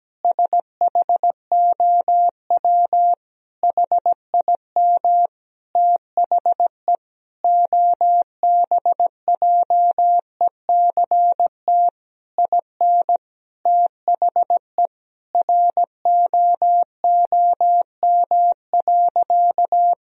Morse code, heard as a keyed tone, about 17 wpm, 700 Hz. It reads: SHOW HIM THE OBJECT IN THE ROOM.
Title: Sentences from Top 1000 Words - Review Part 3 - 17wpm